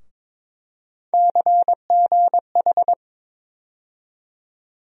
Morse code, heard as a keyed tone, about 22 wpm, 700 Hz. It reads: CGH
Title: USA Call Sign Postfixes - 3 Letters 22wpm